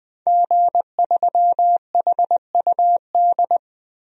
Morse code, heard as a keyed tone, about 20 wpm, 700 Hz. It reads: G3HUD
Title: DXCC-100 Call Signs - 5 Characters - 20wpm